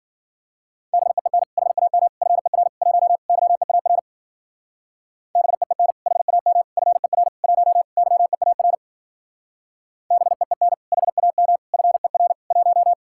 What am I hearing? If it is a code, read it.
Morse code, heard as a keyed tone, about 49 wpm, 700 Hz. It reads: BEEN HAM FER 1 YEAR BEEN HAM FER 1 YEAR BEEN HAM FER 1
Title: Speed-Racing Common QSO Phrases Encore - 40wpm